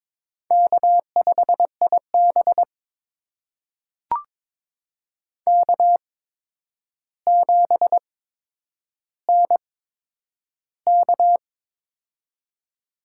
Morse code, heard as a keyed tone, about 22 wpm, spaced as for 9 wpm, 700 Hz. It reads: K5IB K7NK
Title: USA Call Signs - 4 Characters - 8x Character Spacing - 22wpm